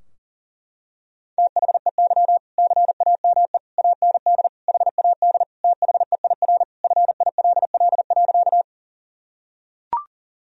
Morse code, heard as a keyed tone, about 40 wpm, 700 Hz. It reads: THEY CAME AND HAD THEIR FILL.
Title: Sentences from Top 1000 Words - Review Part 3 - 40wpm